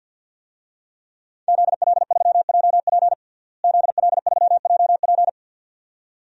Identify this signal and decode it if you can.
Morse code, heard as a keyed tone, about 52 wpm, 700 Hz. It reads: ZL3JP ZL3JP
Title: DXCC-100 Call Signs - 5 Characters - Repeated 2x - 50wpm